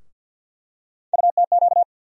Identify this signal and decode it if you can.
Morse code, heard as a keyed tone, about 50 wpm, 700 Hz. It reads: UTQ
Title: USA Call Sign Postfixes - 3 Letters 50wpm